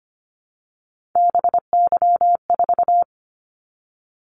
Morse code, heard as a keyed tone, about 25 wpm, 700 Hz. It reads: BY4
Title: Callsign Prefixes for Top 100 DXCC - 25wpm